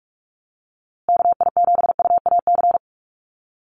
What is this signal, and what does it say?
Morse code, heard as a keyed tone, about 44 wpm, 700 Hz. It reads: KI7UAC
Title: USA Call Signs Repeated 3x 45wpm